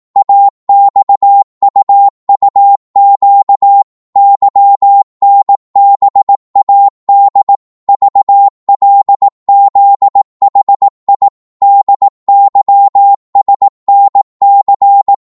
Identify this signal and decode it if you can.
Morse code, heard as a keyed tone, about 18 wpm, 800 Hz. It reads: AXUUQ YNBAD VLZHI DYSNC
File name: Morse / ltr-18wpm